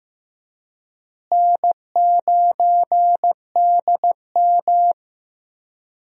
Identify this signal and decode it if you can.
Morse code, heard as a keyed tone, about 15 wpm, 700 Hz. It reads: N9DM
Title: USA Call Signs - 4 Characters - Repeated 2x - 15wpm